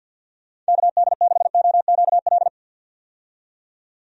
Morse code, heard as a keyed tone, about 51 wpm, 700 Hz. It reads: KD6OQL
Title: USA Call Signs - 6 Characters - 50wpm